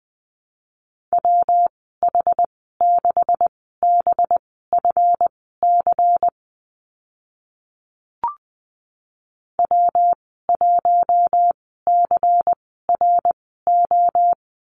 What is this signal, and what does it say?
Morse code, heard as a keyed tone, about 20 wpm, spaced as for 14 wpm, 700 Hz. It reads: WH6BFC W1CRO
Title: 193 Lesson 31g - Letter v - Call Signs 20wpm